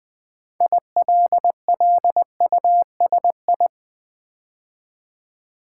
Morse code, heard as a keyed tone, about 20 wpm, 700 Hz. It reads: ILLUSI
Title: Build A Word - 2 Letters At A Time - 8-Letter Words - 20wpm